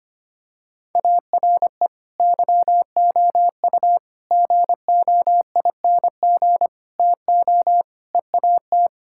Morse code, heard as a keyed tone, about 25 wpm, 700 Hz. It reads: ARE YOU GOING TO EAT
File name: practice / 25wpm / Sentences with 4-Word Phrases - 25wpm